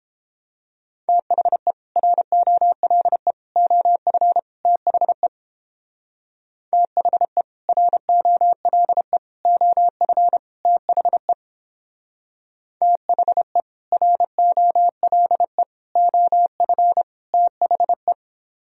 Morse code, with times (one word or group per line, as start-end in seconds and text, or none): 1.1-1.7 THE
2.0-3.3 ROLE
3.6-4.4 OF
4.7-5.3 THE
6.7-7.4 THE
7.7-9.2 ROLE
9.5-10.4 OF
10.7-11.3 THE
12.8-13.6 THE
13.9-15.6 ROLE
16.0-17.0 OF
17.3-18.1 THE